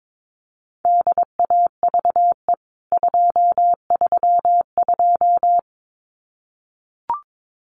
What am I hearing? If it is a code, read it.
Morse code, heard as a keyed tone, about 22 wpm, 700 Hz. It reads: DAVE 232
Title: CWT Exchange 22wpm